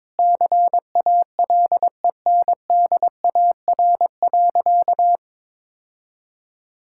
Morse code, heard as a keyed tone, about 22 wpm, 700 Hz. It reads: CALENDAR.
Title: Sentences from Top 1600 Words 22wpm